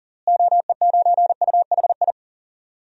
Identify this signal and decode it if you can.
Morse code, heard as a keyed tone, about 40 wpm, 700 Hz. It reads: OE9UHI
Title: DXCC-100 Call Signs - 6 Characters - Repeated 2x - 40wpm